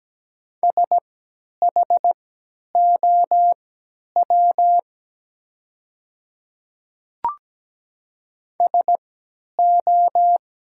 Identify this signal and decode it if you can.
Morse code, heard as a keyed tone, about 17 wpm, spaced as for 10 wpm, 690 Hz. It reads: SHOW SO
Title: 3x Character Spacing - Top 200 Words - 17wpm